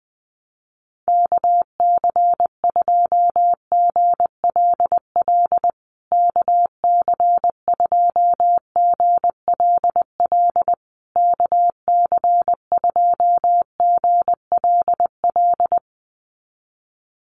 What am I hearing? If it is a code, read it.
Morse code, heard as a keyed tone, about 20 wpm, 700 Hz. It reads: KC2GLL KC2GLL KC2GLL
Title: USA Call Signs Repeated 3x 20wpm